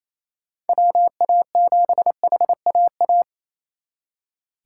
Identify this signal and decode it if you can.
Morse code, heard as a keyed tone, about 37 wpm, 700 Hz. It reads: WA7HAA